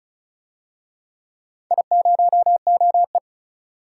Morse code, heard as a keyed tone, about 36 wpm, 700 Hz. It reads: I0OE
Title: DXCC-100 Call Signs - 4 Characters - 35wpm